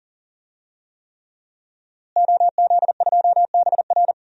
Morse code, heard as a keyed tone, about 40 wpm, 700 Hz. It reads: OZ2BR
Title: DXCC-100 Call Signs - 5 Characters - Repeated 3x - 40wpm